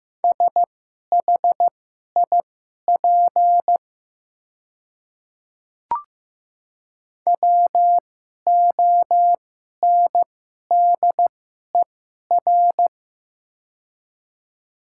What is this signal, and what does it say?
Morse code, heard as a keyed tone, about 15 wpm, spaced as for 11 wpm, 700 Hz. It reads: SHIP WONDER
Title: 2x Character Spacing - Top 500 Words - 15wpm